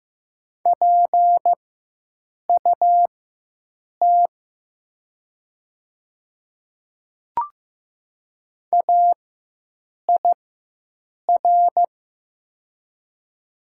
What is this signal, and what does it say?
Morse code, heard as a keyed tone, about 15 wpm, spaced as for 7 wpm, 700 Hz. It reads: PUT AIR
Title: 4x Character Spacing - Top 200 Words - 15wpm